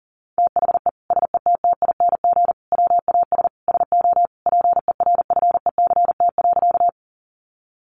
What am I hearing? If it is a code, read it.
Morse code, heard as a keyed tone, about 40 wpm, 700 Hz. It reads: THE SETTING WAS SO PERFECT.
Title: Sentences from Top 1300 Words 40wpm